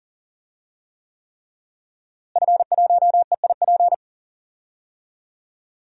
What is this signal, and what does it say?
Morse code, heard as a keyed tone, about 40 wpm, 700 Hz. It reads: F1EIP
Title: DXCC-100 Call Signs - 5 Characters - Repeated 3x - 40wpm